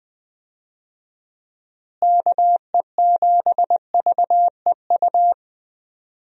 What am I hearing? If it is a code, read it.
Morse code, heard as a keyed tone, about 20 wpm, 700 Hz. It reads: KE7VEU
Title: USA Call Signs - 6 Characters - Repeated 2x - 20wpm